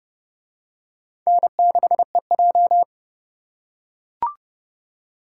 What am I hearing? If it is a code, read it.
Morse code, heard as a keyed tone, about 29 wpm, 700 Hz. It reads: N6EJ